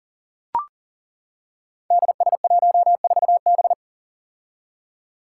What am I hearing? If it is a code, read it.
Morse code, heard as a keyed tone, about 42 wpm, 700 Hz. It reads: DS14B
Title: Five Challenging Letter-Number Combinations 40wpm